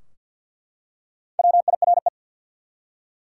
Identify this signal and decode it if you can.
Morse code, heard as a keyed tone, about 47 wpm, 700 Hz. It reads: WIRE
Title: Top 180 QSO Elements 50wpm